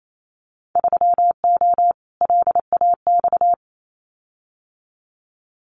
Morse code, heard as a keyed tone, about 28 wpm, 700 Hz. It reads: 3O LAX
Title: Winter Field Day Exchanges Encore 28wpm